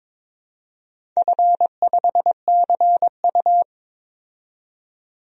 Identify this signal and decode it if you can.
Morse code, heard as a keyed tone, about 23 wpm, 700 Hz. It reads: F5CU
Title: DXCC-100 Call Signs - 4 Characters - 22wpm